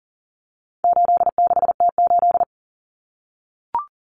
Morse code, heard as a keyed tone, about 40 wpm, 700 Hz. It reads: 86T8
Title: Common Cut Numbers - 4-Digit 40wpm